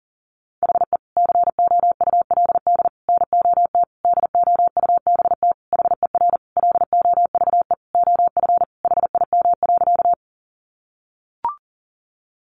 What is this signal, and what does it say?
Morse code, heard as a keyed tone, about 40 wpm, 700 Hz. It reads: HE COULD NOT DOUBT HER LOVE OF HIM.
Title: Sentences from Top 1500 Words 40wpm